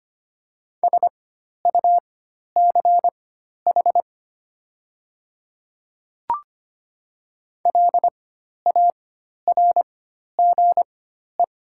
Morse code, heard as a keyed tone, about 25 wpm, spaced as for 12 wpm, 700 Hz. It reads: SUCH LARGE
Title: 4x Character Spacing - Top 200 Words - 25wpm